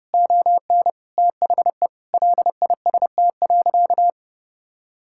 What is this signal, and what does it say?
Morse code, heard as a keyed tone, about 32 wpm, 700 Hz. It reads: ON THE LIST.
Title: Sentences from Top 1000 Words - Review Part 6 - 30wpm